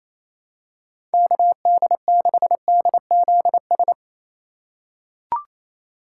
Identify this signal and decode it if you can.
Morse code, heard as a keyed tone, about 28 wpm, 700 Hz. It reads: KD6DZS
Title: USA Call Signs - 6 Characters - 3x Character Spacing - 28wpm